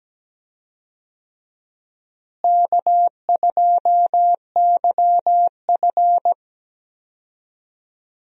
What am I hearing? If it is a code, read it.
Morse code, heard as a keyed tone, about 17 wpm, 700 Hz. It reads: K2YF
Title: USA Call Signs - 4 Characters - Repeated 2x - 17wpm